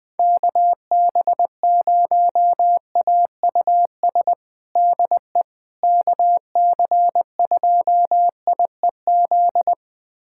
Morse code, heard as a keyed tone, about 20 wpm, 700 Hz. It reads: KB0AUS DE KC2IEZ
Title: USA Callsign DE Callsign 20wpm